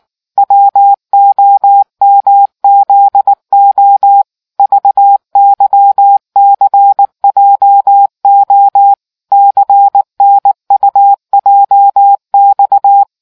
Morse code, heard as a keyed tone, about 19 wpm, 800 Hz. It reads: WOMZO VYCJO CNUJX